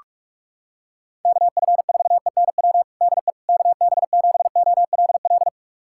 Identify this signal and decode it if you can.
Morse code, heard as a keyed tone, about 46 wpm, 700 Hz. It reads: KF4ENW DE KB7OLL
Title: USA Callsign DE Callsign - 6 Characters - 45wpm